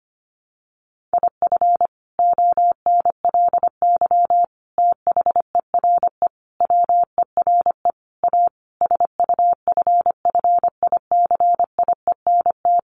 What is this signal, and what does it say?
Morse code, heard as a keyed tone, about 25 wpm, 700 Hz. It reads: IF ONLY THERE WERE A SUFFICIENT
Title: Sentences from Top 1400 Words 25wpm